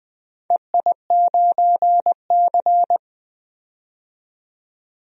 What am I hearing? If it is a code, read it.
Morse code, heard as a keyed tone, about 20 wpm, 700 Hz. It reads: EI9C
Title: DXCC-100 Call Signs - 4 Characters - 20wpm